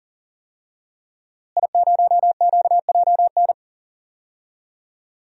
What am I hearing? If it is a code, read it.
Morse code, heard as a keyed tone, about 40 wpm, 700 Hz. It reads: I0QJN
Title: DXCC-100 Call Signs - 5 Characters - Repeated 3x - 40wpm